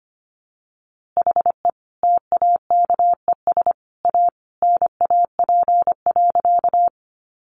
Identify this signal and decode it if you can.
Morse code, heard as a keyed tone, about 25 wpm, 700 Hz. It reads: HE TAKES A NAP.
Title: Easy Phrases 25wpm